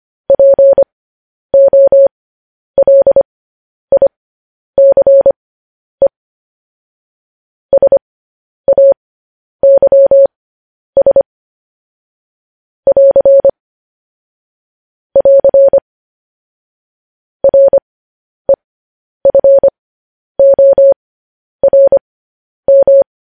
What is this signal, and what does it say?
Morse code, heard as a keyed tone, about 25 wpm, spaced as for 10 wpm, 550 Hz. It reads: POLICE SAYS <AR> <AR> REFORM
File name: podcast / News headlines-1x-25x10